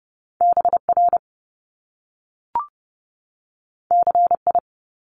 Morse code, heard as a keyed tone, about 30 wpm, 700 Hz. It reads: BR CI
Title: Top Two Letter Combinations in Words 30wpm